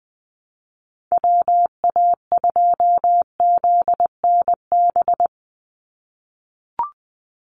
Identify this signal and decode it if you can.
Morse code, heard as a keyed tone, about 20 wpm, 700 Hz. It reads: WA2ZNB
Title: USA Call Signs 20wpm